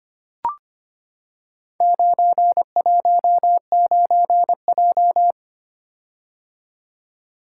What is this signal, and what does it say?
Morse code, heard as a keyed tone, about 25 wpm, 700 Hz. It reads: 919J
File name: practice / 25wpm / Four Letter-Number Combinations - 25wpm